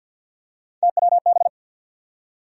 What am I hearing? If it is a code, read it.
Morse code, heard as a keyed tone, about 48 wpm, 700 Hz. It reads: TWB
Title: USA Call Sign Postfixes - 3 Letters 50wpm